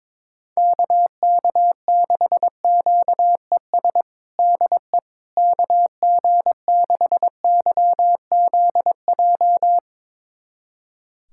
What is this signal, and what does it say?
Morse code, heard as a keyed tone, about 22 wpm, 700 Hz. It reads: KK6QES DE KG6YZJ
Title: USA Callsign DE Callsign - 6 Characters - 22wpm